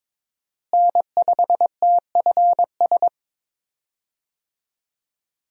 Morse code, heard as a keyed tone, about 22 wpm, 700 Hz. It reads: N5TFS
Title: USA Call Signs - 5 Characters - Repeated 2x - 22wpm